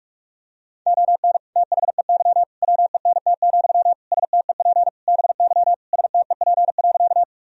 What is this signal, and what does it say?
Morse code, heard as a keyed tone, about 46 wpm, 700 Hz. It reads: ON THEY WENT, STEP BY STEP.
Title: Sentences from Top 1000 Words - Review Part 2 - 45wpm